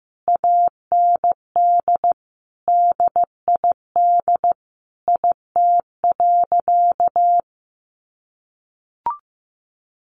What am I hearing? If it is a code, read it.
Morse code, heard as a keyed tone, about 15 wpm, 700 Hz. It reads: AND DID IT.
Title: Sentences from Top 1500 Words 15wpm